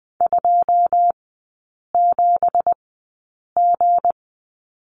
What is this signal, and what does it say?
Morse code, heard as a keyed tone, about 20 wpm, spaced as for 8 wpm, 700 Hz. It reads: 27G
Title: 166 Lesson 27e - Number 7 - Three Characters 20wpm